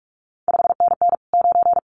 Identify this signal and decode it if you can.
Morse code, heard as a keyed tone, about 46 wpm, 700 Hz. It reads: 5NN 9
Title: CQ WW DX Contest Exchange - 45wpm